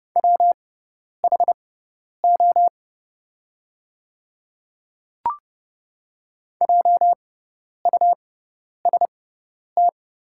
Morse code, heard as a keyed tone, about 29 wpm, spaced as for 10 wpm, 700 Hz. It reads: WHO JUST
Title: 6x Character Spacing - Top 300 Words - 30wpm